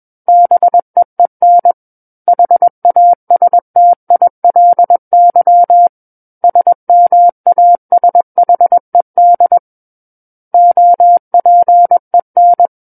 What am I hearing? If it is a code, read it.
Morse code, heard as a keyed tone, about 21 wpm, 700 Hz. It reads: BEEN HASTILY SMASHED OPEN
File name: Book One / 12-Chapter 12 - 21 WPM